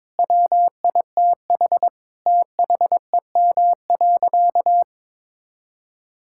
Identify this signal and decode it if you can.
Morse code, heard as a keyed tone, about 22 wpm, 700 Hz. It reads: WITH THEM.
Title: Sentences from Top 1000 Words - Review Part 6 - 22wpm